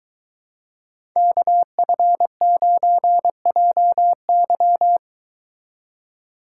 Morse code, heard as a keyed tone, about 22 wpm, 700 Hz. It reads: KF9JY